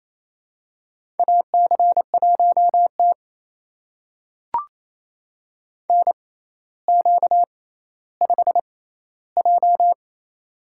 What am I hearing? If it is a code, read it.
Morse code, heard as a keyed tone, about 28 wpm, spaced as for 14 wpm, 700 Hz. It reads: AC1T NQ5J
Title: USA Call Signs - 4 Characters - 6x Character Spacing - 28wpm